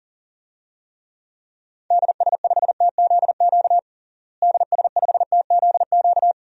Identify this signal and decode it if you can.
Morse code, heard as a keyed tone, about 37 wpm, 700 Hz. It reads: DS5TZQ DS5TZQ
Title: DXCC-100 Call Signs - 6 Characters - Repeated 3x - 40wpm